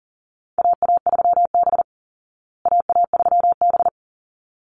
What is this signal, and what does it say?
Morse code, heard as a keyed tone, about 40 wpm, 700 Hz. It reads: AA3B AA3B
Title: Call Signs Repeated 3x - 40wpm